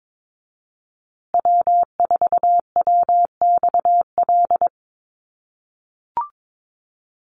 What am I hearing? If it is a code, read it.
Morse code, heard as a keyed tone, about 22 wpm, 700 Hz. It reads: W4WXL
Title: USA Call Signs Repeated 2x 22wpm